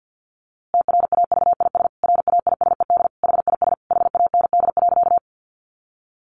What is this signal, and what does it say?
Morse code, heard as a keyed tone, about 50 wpm, 700 Hz. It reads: TRAVIS RAISED HIS HAND.
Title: A Sound Of Thunder 50wpm